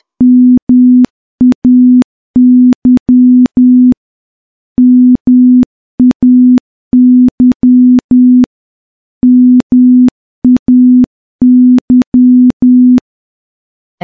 Morse code, heard as a keyed tone, about 10 wpm, 250 Hz. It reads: MAY MAY MAY
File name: morse-training-10WPM